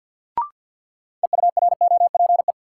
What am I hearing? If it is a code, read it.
Morse code, heard as a keyed tone, about 46 wpm, 700 Hz. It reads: EUROPE
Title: Sets of 5 Words Encore - New Word Drill - 50wpm